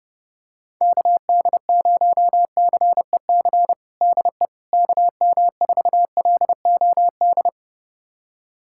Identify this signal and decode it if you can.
Morse code, heard as a keyed tone, about 30 wpm, 700 Hz. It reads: KD0CEC DE KM4LOD
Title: USA Callsign DE Callsign - 6 Characters - 30wpm